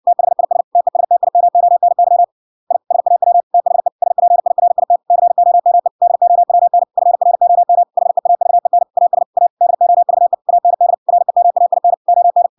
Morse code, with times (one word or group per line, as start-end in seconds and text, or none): 0.1-0.6 THIS
0.8-2.2 TESTIMONY
2.7-2.8 I
2.9-3.4 SAW
3.5-3.9 THE
4.0-5.0 SPIRIT
5.1-5.9 COME
6.0-6.8 DOWN
7.0-7.8 FROM
8.0-8.8 HEAVEN
9.0-9.2 AS
9.4-9.5 A
9.6-10.3 DOVE
10.5-10.9 AND
11.1-11.9 REMAIN
12.1-12.5 ON